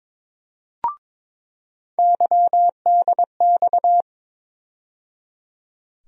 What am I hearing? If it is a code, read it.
Morse code, heard as a keyed tone, about 22 wpm, 700 Hz. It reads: YDX
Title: USA Call Sign Postfixes - 3 Letters 22wpm